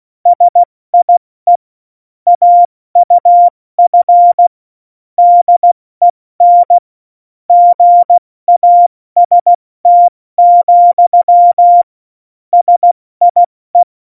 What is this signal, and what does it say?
Morse code, heard as a keyed tone, about 16 wpm, 700 Hz. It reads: SIE AUF DEN GAST, SIE
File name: Morsekurs / Witze / Witz 15. Oktober 2019, 70 F80